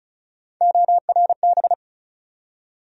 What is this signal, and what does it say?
Morse code, heard as a keyed tone, about 36 wpm, 700 Hz. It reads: ORB
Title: USA Call Sign Postfixes - 3 Letters 35wpm